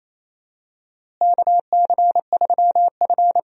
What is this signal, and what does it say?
Morse code, heard as a keyed tone, about 28 wpm, 700 Hz. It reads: KC3F
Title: USA Call Signs - 4 Characters - Repeated 3x - 28wpm